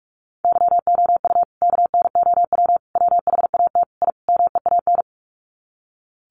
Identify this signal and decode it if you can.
Morse code, heard as a keyed tone, about 44 wpm, 700 Hz. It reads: YOU KNOW WHAT I MEAN
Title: Sets of 5 Words 45wpm